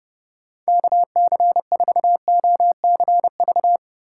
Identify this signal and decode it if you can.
Morse code, heard as a keyed tone, about 30 wpm, 700 Hz. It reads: KC4OCV
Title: USA Call Signs - 6 Characters - 3x Character Spacing - 30wpm